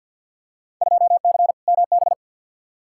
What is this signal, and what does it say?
Morse code, heard as a keyed tone, about 47 wpm, 700 Hz. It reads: 2C MB
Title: Field Day Exchanges Encore - 50wpm